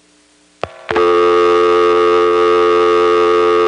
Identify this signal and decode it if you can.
Dial tone (North America), 350 + 440 Hz, with noise.
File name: phonetrips / EvanNX11HQ